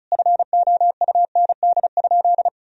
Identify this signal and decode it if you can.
Morse code, heard as a keyed tone, about 36 wpm, 700 Hz. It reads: FOUND?
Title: Sentences from Top 300 Words and Abbreviations 35wpm